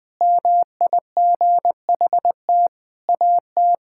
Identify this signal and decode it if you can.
Morse code, heard as a keyed tone, about 20 wpm, 700 Hz. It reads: MIGHT AT
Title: Sentences from Top 1000 Words - Review Part 6 - 20wpm